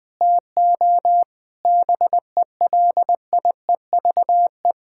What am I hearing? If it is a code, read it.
Morse code, heard as a keyed tone, about 20 wpm, 700 Hz. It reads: TO BELIEVE
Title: Sentences with 6-Word Phrases - 20wpm